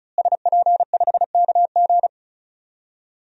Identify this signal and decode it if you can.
Morse code, heard as a keyed tone, about 34 wpm, 700 Hz. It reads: SP5KG
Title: DXCC-100 Call Signs - 5 Characters - Repeated 2x - 35wpm